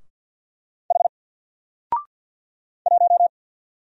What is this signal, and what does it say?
Morse code, heard as a keyed tone, about 50 wpm, 700 Hz. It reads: H 1
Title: Single Letter-Number 50wpm